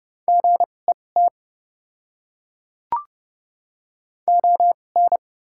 Morse code, heard as a keyed tone, about 30 wpm, spaced as for 22 wpm, 700 Hz. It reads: GET ON
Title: 2x Character Spacing - Top 200 Words - 30wpm